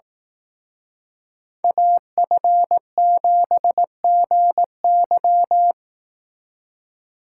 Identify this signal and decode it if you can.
Morse code, heard as a keyed tone, about 18 wpm, 700 Hz. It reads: AF7GY
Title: Speed Racing - USA 2x2 Call Signs 15wpm